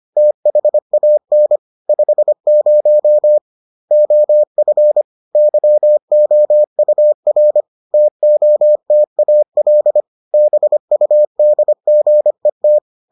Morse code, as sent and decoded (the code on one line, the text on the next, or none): - .... .- -. / ..... ----- / --- ..-. / -.-- --- ..- .-. / - --- - .- .-.. / -... ..- -.. --. -
THAN 50 OF YOUR TOTAL BUDGET